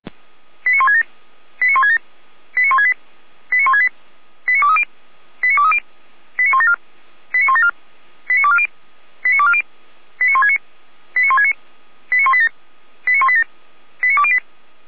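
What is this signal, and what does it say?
Five-tone selective call (ZVEI), heard as quick runs of five tones, tones of about 80 ms. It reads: 89167 89167 89168 89168 89230 89230 89174 89174 89240 89240 89169 89169 89187 89187 89208
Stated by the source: ZVEI five-tone call received by SDR.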